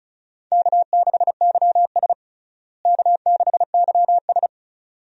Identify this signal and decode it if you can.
Morse code, heard as a keyed tone, about 36 wpm, 700 Hz. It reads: K6YS K6YS
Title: USA Call Signs - 4 Characters - Repeated 2x - 35wpm